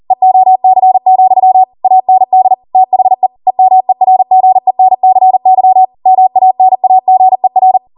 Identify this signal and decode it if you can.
Morse code, heard as a keyed tone, about 40 wpm, 750 Hz. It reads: EOC, AND THE EMERGENCY MANAGER
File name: Morse / 40WPM